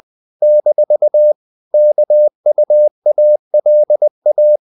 Morse code, heard as a keyed tone, about 20 wpm, 600 Hz. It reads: - KUALA
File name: CW / Practice / NationalCapitals20